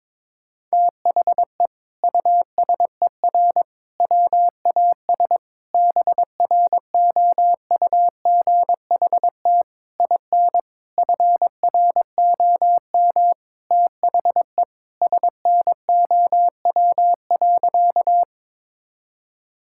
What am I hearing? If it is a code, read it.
Morse code, heard as a keyed tone, about 22 wpm, 700 Hz. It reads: THE USER WAS BROUGHT IN FROM THE SNOW.
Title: Sentences from Top 1000 Words - Review Part 5 - 22wpm